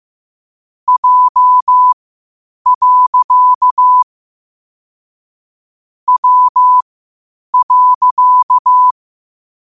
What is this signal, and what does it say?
Morse code, heard as a keyed tone, about 15 wpm, spaced as for 8 wpm, 1000 Hz. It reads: J. W.
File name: qotdfiles / 8wpm